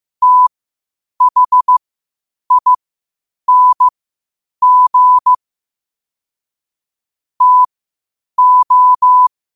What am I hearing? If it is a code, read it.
Morse code, heard as a keyed tone, about 15 wpm, spaced as for 8 wpm, 1000 Hz. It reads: THING TO
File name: qotdfiles / 8wpm